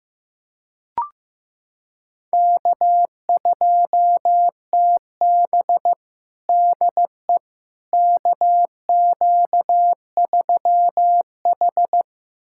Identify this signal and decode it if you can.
Morse code, heard as a keyed tone, about 15 wpm, 700 Hz. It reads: K2TB DE KQ3H
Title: USA Callsign DE Callsign - 4 Characters - 15wpm